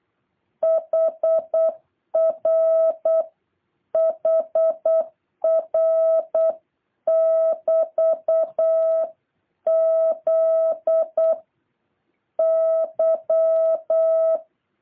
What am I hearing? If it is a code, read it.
Morse code, heard as a keyed tone, about 8 wpm, 650 Hz. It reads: HR HR <BT> Z Y
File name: jh8cbh / 2023cw